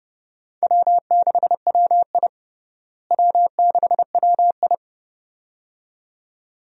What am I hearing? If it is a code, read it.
Morse code, heard as a keyed tone, about 30 wpm, 700 Hz. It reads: W6WI W6WI
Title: USA Call Signs - 4 Characters - Repeated 2x - 30wpm